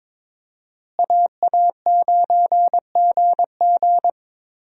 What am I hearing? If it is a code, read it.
Morse code, heard as a keyed tone, about 22 wpm, 700 Hz. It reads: AA9GG
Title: USA Call Signs - 5 Characters - Repeated 2x - 22wpm